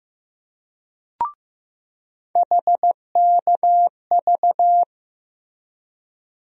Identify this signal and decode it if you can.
Morse code, heard as a keyed tone, about 15 wpm, 700 Hz. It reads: HKV
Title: USA Call Sign Postfixes - 3 Letters 15wpm